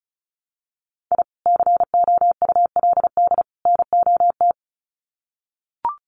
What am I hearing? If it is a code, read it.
Morse code, heard as a keyed tone, about 34 wpm, 700 Hz. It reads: I COULD NOT
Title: Sets of 3 Words 35wpm